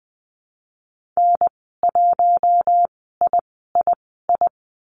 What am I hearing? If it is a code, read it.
Morse code, heard as a keyed tone, about 20 wpm, spaced as for 15 wpm, 700 Hz. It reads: N1III
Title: 055 Lesson 10g - Letter r - Call Signs 20wpm